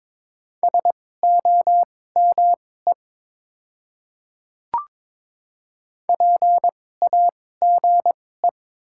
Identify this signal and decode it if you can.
Morse code, heard as a keyed tone, about 22 wpm, spaced as for 16 wpm, 700 Hz. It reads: SOME PAGE
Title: 2x Character Spacing - Top 500 Words - 22wpm